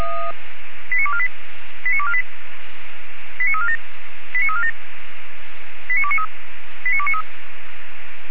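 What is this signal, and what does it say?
Five-tone selective call (ZVEI), heard as quick runs of five tones, tones of about 70 ms. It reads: 89248 89248 89347 89347 89293 89293
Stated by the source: ZVEI five-tone selective call received by SDR.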